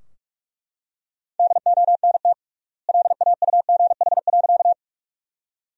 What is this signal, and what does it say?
Morse code, heard as a keyed tone, about 42 wpm, 700 Hz. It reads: DONT LAUGH.
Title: 3x Word-Spacing - Sentences from Top 500 Words - 45wpm